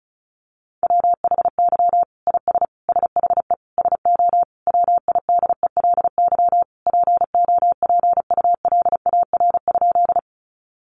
Morse code, heard as a keyed tone, about 34 wpm, 700 Hz. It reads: WHY IS SHE SO WIDELY POPULAR?
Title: Sentences from Top 1200 Words 35wpm